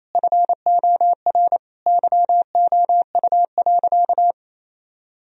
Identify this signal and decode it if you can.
Morse code, heard as a keyed tone, about 28 wpm, 700 Hz. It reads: FOR YOU.
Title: Sentences from Top 1000 Words - Review Part 2 - 28wpm